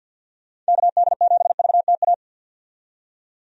Morse code, heard as a keyed tone, about 46 wpm, 700 Hz. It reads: KD7VTA
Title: USA Call Signs - 6 Characters - 50wpm